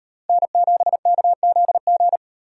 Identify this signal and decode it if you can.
Morse code, heard as a keyed tone, about 38 wpm, 700 Hz. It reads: N7KZG